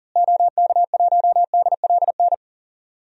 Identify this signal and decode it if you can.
Morse code, heard as a keyed tone, about 40 wpm, 700 Hz. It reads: OK1DLN
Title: DXCC-100 Call Signs - 6 Characters - Repeated 3x - 40wpm